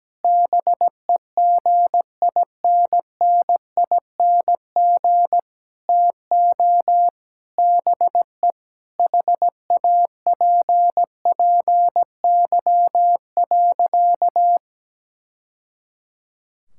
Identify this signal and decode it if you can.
Morse code, heard as a keyed tone, about 17 wpm, 700 Hz. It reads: BEGINNING TO BE HAPPY.
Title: Sentences from Top 1000 Words - Review Part 5 - 17wpm